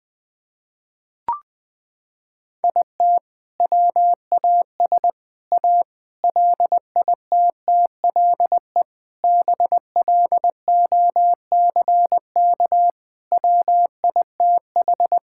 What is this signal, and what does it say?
Morse code, heard as a keyed tone, about 20 wpm, 700 Hz. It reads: IT WAS A LITTLE BLOCK WITH
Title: Sentences from Top 1600 Words 20wpm